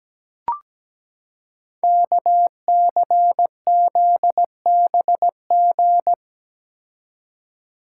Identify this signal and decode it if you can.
Morse code, heard as a keyed tone, about 17 wpm, 700 Hz. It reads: KCZBG
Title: Code Groups - 5 Letters - 17wpm